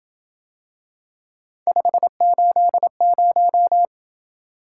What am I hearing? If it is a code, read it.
Morse code, heard as a keyed tone, about 27 wpm, 700 Hz. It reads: 580